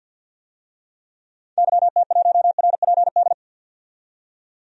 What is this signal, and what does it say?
Morse code, heard as a keyed tone, about 51 wpm, 700 Hz. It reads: YT1RPD
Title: DXCC-100 Call Signs - 6 Characters - 50wpm